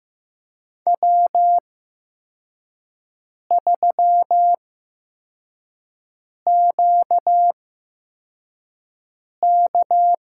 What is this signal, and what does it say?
Morse code, heard as a keyed tone, about 15 wpm, spaced as for 4 wpm, 700 Hz. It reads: W3QK
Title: USA Call Signs - 8x Character Spacing - 15wpm